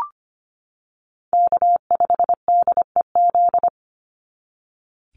Morse code, heard as a keyed tone, about 25 wpm, 700 Hz. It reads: K5DEZ
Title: NA SOTA Chaser Callsigns 25wpm